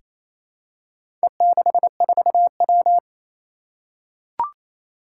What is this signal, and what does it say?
Morse code, heard as a keyed tone, about 27 wpm, 700 Hz. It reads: E64W
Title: Four Challenging Letter-Number Combinations 28wpm